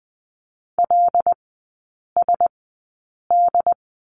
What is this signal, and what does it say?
Morse code, heard as a keyed tone, about 20 wpm, spaced as for 8 wpm, 700 Hz. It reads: LSD
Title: 081 Lesson 14e - Number 2 - Three Characters 20wpm